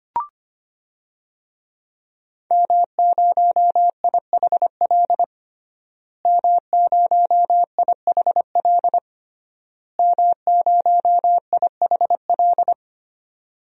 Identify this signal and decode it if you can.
Morse code, heard as a keyed tone, about 25 wpm, 700 Hz. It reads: M0IHL M0IHL M0IHL
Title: DXCC-100 Call Signs - 5 Characters - Repeated 3x - 25wpm